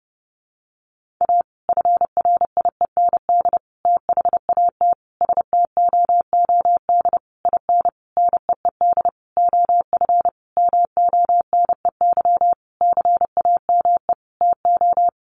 Morse code, heard as a keyed tone, about 28 wpm, 700 Hz. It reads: A FRIEND THAT STOOD IN NEED OF MONEY CAME TO
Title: Sentences from Top 1500 Words 30wpm